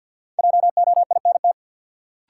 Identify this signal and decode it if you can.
Morse code, heard as a keyed tone, about 46 wpm, 700 Hz. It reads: JOINT
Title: Sentences from Top 1300 Words - New Word Drill - 50wpm